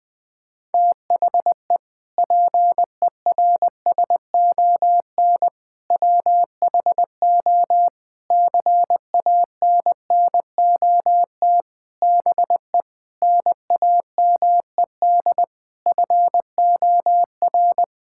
Morse code, heard as a keyed tone, about 19 wpm, 700 Hz. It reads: THE PERSON WHO CANNOT BE NAMED FOR
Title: Sentences with 8-Word Phrases - Spoken Beginning - 20wpm